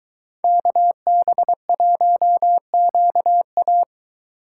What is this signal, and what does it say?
Morse code, heard as a keyed tone, about 22 wpm, 700 Hz. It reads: KB1QA